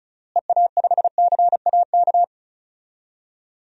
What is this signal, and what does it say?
Morse code, heard as a keyed tone, about 34 wpm, 700 Hz. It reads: EA5CAK
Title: DXCC-100 Call Signs - 6 Characters - Repeated 3x - 35wpm